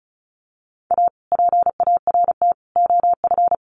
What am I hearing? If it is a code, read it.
Morse code, heard as a keyed tone, about 34 wpm, 700 Hz. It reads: A PART OF
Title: Sets of 3 Words 35wpm